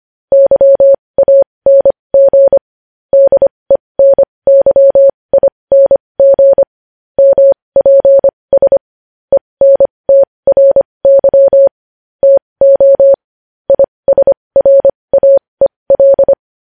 Morse code, heard as a keyed tone, about 25 wpm, spaced as for 20 wpm, 550 Hz. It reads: YANG DENYING MPS ENTRY TO ISRAEL
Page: news headlines in morse code at 20 wpm